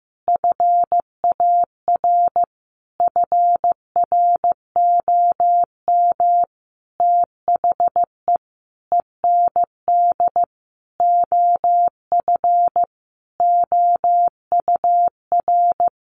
Morse code, heard as a keyed tone, about 15 wpm, 700 Hz. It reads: FAR FROM THE END OF OUR
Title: Sentences from Top 1300 Words 15wpm